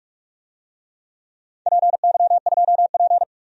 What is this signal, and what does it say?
Morse code, heard as a keyed tone, about 44 wpm, 700 Hz. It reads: PY2P
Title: DXCC-100 Call Signs - 4 Characters - 45wpm